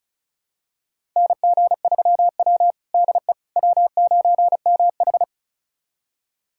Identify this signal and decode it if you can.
Morse code, heard as a keyed tone, about 36 wpm, 700 Hz. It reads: NG3W DE W9MH
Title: USA Callsign DE Callsign - 4 Characters - 35wpm